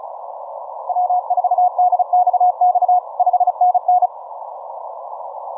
Morse code, heard as a keyed tone, about 36 wpm, 720 Hz. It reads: W4DXX 5NN
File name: simple-file-list / 6O6O 15MCW